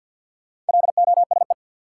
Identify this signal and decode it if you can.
Morse code, heard as a keyed tone, about 50 wpm, 700 Hz. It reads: LOSE
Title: Sentences from Top 1100 Words - New Word Drill - 50wpm